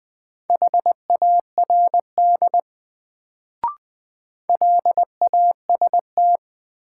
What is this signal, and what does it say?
Morse code, heard as a keyed tone, about 20 wpm, 700 Hz. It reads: HARD LAST
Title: Sentences from Top 300 Words - New Word Drill - 20wpm